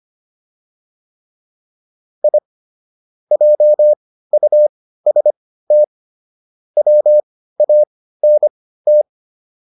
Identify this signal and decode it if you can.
Morse code, heard as a keyed tone, about 25 wpm, spaced as for 15 wpm, 600 Hz. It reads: I JUST WANT